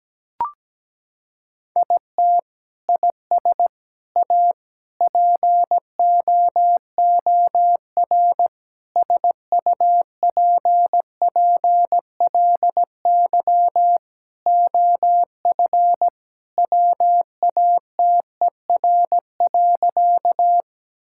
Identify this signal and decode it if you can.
Morse code, heard as a keyed tone, about 17 wpm, 700 Hz. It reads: IT IS A POOR SUPPLY OF WATER.
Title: Sentences from Top 1000 Words - Review Part 5 - 17wpm